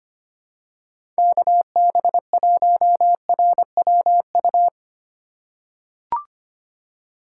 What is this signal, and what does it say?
Morse code, heard as a keyed tone, about 25 wpm, 700 Hz. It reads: KB1RWU
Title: USA Call Signs - 6 Characters - 4x Character Spacing - 25wpm